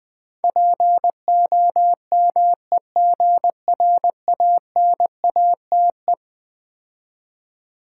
Morse code, heard as a keyed tone, about 20 wpm, 700 Hz. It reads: POMEGRANATE
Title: Difficult Words with Hints - 20wpm